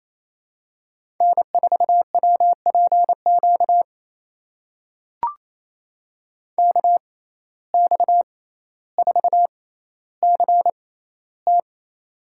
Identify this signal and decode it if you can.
Morse code, heard as a keyed tone, about 28 wpm, spaced as for 14 wpm, 700 Hz. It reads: N4WPQ KX4CT
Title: USA Call Signs - 5 Characters - 6x Character Spacing - 28wpm